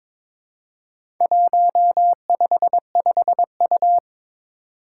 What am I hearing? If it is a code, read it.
Morse code, heard as a keyed tone, about 22 wpm, 700 Hz. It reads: 155U
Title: Four Challenging Letter-Number Combinations 22wpm